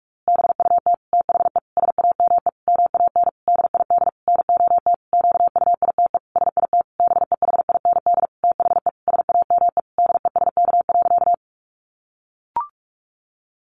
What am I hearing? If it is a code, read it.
Morse code, heard as a keyed tone, about 44 wpm, 700 Hz. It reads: BUT THE SAME MAN DID NOT QUITE SIT BEHIND THE SAME DESK.
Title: A Sound Of Thunder 45wpm